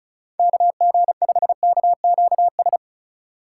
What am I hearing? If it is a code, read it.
Morse code, heard as a keyed tone, about 36 wpm, 700 Hz. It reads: KG5KQS
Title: USA Call Signs - 6 Characters - 2x Character Spacing - 35wpm